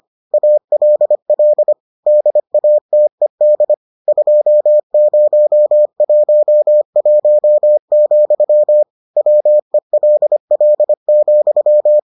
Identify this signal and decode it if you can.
Morse code, heard as a keyed tone, about 25 wpm, 600 Hz. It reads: ALL DATED 2011, WELL,